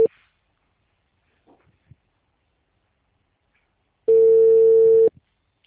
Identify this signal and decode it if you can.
Ringback tone (China, 450 Hz).